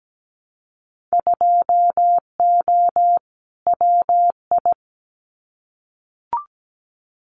Morse code, heard as a keyed tone, about 17 wpm, 700 Hz. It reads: 2O WI
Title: Winter Field Day Exchanges 17wpm